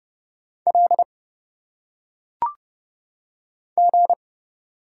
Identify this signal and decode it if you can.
Morse code, heard as a keyed tone, about 30 wpm, 700 Hz. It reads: L G
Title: Single Letters - Phonetic Pronunciation 30wpm